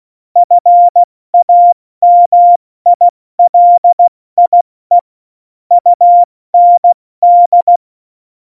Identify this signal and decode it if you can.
Morse code, heard as a keyed tone, about 16 wpm, 700 Hz. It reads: FAMILIE UND
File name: Morsekurs / Witze / Witz 19. Februar 2019, 70 F80